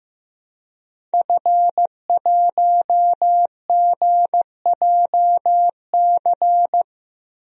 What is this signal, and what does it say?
Morse code, heard as a keyed tone, about 15 wpm, 700 Hz. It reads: F1GJC
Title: DXCC-100 Call Signs - 5 Characters - Repeated 3x - 15wpm